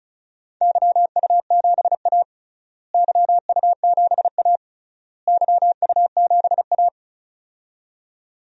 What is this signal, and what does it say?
Morse code, heard as a keyed tone, about 34 wpm, 700 Hz. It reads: YU7A YU7A YU7A
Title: DXCC-100 Call Signs - 4 Characters - Repeated 3x - 35wpm